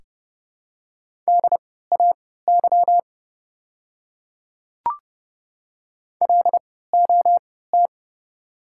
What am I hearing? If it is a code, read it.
Morse code, heard as a keyed tone, about 30 wpm, spaced as for 17 wpm, 700 Hz. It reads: DAY LOT
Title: 3x Character Spacing - Top 500 Words - 30wpm